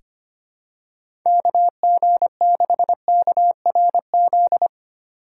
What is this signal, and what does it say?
Morse code, heard as a keyed tone, about 25 wpm, 700 Hz. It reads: KG6KRZ
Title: USA Call Signs - 6 Characters - 3x Character Spacing - 25wpm